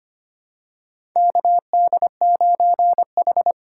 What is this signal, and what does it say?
Morse code, heard as a keyed tone, about 25 wpm, 700 Hz. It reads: KD9H
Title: USA Call Signs - 4 Characters - 3x Character Spacing - 25wpm